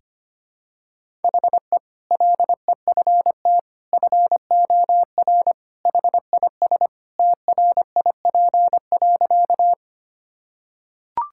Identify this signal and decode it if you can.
Morse code, heard as a keyed tone, about 25 wpm, 700 Hz. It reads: HE LEFT FOR HIS TRIP.
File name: practice / 25wpm / Sentences with 2-Word Phrases - 25wpm